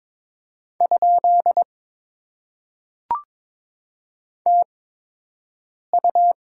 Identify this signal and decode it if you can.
Morse code, heard as a keyed tone, about 22 wpm, spaced as for 6 wpm, 700 Hz. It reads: ? TU
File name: Top 90 QSO Elements - 8x Character Spacing - 22wpm